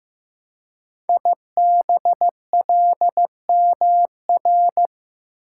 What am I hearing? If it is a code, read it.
Morse code, heard as a keyed tone, about 15 wpm, 700 Hz. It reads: IBLMR
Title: Code Groups - 5 Letters - 15wpm